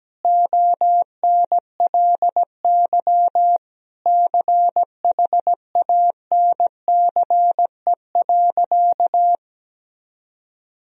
Morse code, heard as a keyed tone, about 17 wpm, 700 Hz. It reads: ONLY CHANCE.
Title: Sentences from Top 1000 Words - Review Part 4 - 17wpm